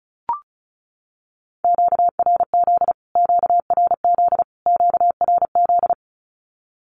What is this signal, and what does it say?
Morse code, heard as a keyed tone, about 34 wpm, 700 Hz. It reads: QRZ QRZ QRZ
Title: Q-Codes Repeated 3x 35wpm